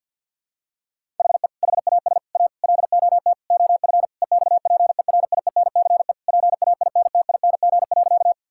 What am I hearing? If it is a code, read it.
Morse code, heard as a keyed tone, about 48 wpm, 700 Hz. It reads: HE HAS A LOT OF EXPERIENCE PAINTING.
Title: Sentences from Top 1000 Words - Review Part 6 - 50wpm